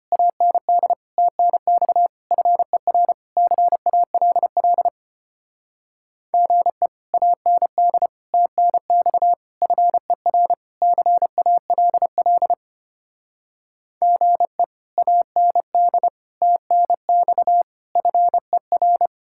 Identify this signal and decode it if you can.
Morse code, heard as a keyed tone, about 33 wpm, 700 Hz. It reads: AND TNX FER CALL GE AND TNX FER CALL GE AND TNX FER